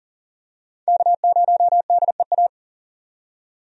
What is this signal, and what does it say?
Morse code, heard as a keyed tone, about 40 wpm, 700 Hz. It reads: K0DEA
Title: USA Call Signs - 5 Characters - 40wpm